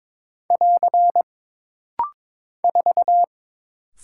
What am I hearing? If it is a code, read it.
Morse code, heard as a keyed tone, about 23 wpm, 700 Hz. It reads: <AR> 4
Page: This rapid-fire timing is geared to push your ICR (Instant Character Recognition) skills.